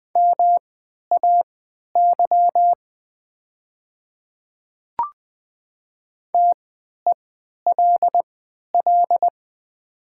Morse code, heard as a keyed tone, about 20 wpm, spaced as for 11 wpm, 700 Hz. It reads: MAY TELL
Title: 3x Character Spacing - Top 200 Words - 20wpm